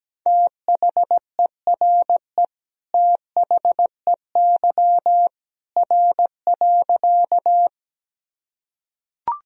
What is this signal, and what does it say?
Morse code, heard as a keyed tone, about 17 wpm, 700 Hz. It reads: THERE THEY R.
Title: Sentences from Top 100 Words and Abbreviations 17wpm